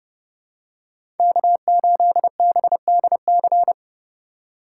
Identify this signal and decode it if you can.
Morse code, heard as a keyed tone, about 30 wpm, 700 Hz. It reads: K8BDC
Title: USA Call Signs - 5 Characters - 4x Character Spacing - 30wpm